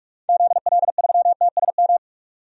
Morse code, heard as a keyed tone, about 46 wpm, 700 Hz. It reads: ZL3TSM
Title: DXCC-100 Call Signs - 6 Characters - 45wpm